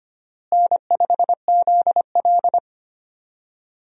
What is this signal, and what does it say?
Morse code, heard as a keyed tone, about 25 wpm, 700 Hz. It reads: N5ZL
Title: USA Call Signs - 4 Characters - Repeated 2x - 25wpm